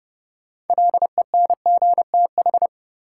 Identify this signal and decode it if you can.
Morse code, heard as a keyed tone, about 30 wpm, 700 Hz. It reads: LENGTH
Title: Sentences from Top 1000 Words - New Word Drill - 30wpm